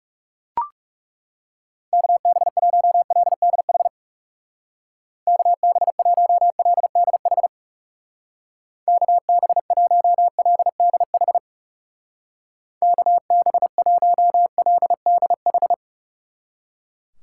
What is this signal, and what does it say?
Morse code, heard as a keyed tone, about 37 wpm, 700 Hz. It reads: KB1LDH KB1LDH KB1LDH KB1LDH
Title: Speed Racing - USA 2x3 Call Signs 30wpm